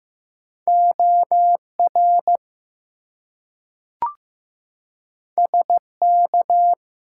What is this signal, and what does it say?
Morse code, heard as a keyed tone, about 15 wpm, 700 Hz. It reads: OR SK
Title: US States and Canadian Provinces 15wpm